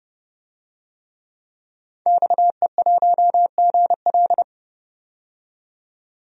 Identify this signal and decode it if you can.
Morse code, heard as a keyed tone, about 30 wpm, 700 Hz. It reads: XE1GL
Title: DXCC-100 Call Signs - 5 Characters - 30wpm